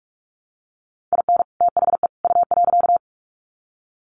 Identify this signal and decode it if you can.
Morse code, heard as a keyed tone, about 44 wpm, 700 Hz. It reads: IN THE U.
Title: Sets of 3 Words 45wpm